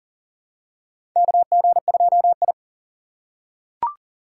Morse code, heard as a keyed tone, about 38 wpm, 700 Hz. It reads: KG2I
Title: USA Call Signs - 4 Characters - 6x Character Spacing - 40wpm